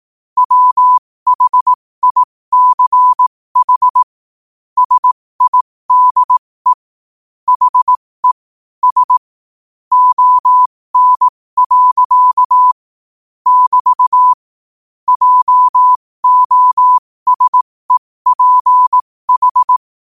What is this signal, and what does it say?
Morse code, heard as a keyed tone, about 18 wpm, spaced as for 15 wpm, 1000 Hz. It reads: WHICH SIDE HE S ON. <BT> JOSEPH
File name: qotdfiles / 15wpm